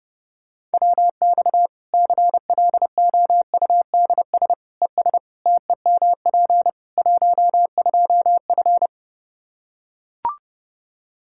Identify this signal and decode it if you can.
Morse code, heard as a keyed tone, about 30 wpm, 700 Hz. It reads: WX CLOUDS ES TEMP 12F
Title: K1USN SST QSO Practice 30wpm